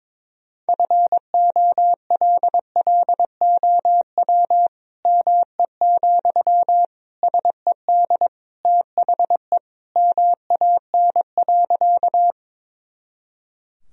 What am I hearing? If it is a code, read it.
Morse code, heard as a keyed tone, about 22 wpm, 700 Hz. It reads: FOLLOW ME, SED THE MAN.
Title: Sentences from Top 200 Words and Abbreviations 22wpm